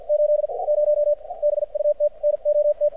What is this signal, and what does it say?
Morse code, heard as a keyed tone, about 54 wpm, spaced as for 41 wpm, 590 Hz. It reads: 80 BUT NOT
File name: archived / CRSnet